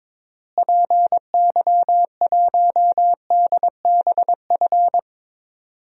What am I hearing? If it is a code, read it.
Morse code, heard as a keyed tone, about 22 wpm, 700 Hz. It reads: PY1DBF
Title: DXCC-100 Call Signs - 6 Characters - Repeated 2x - 22wpm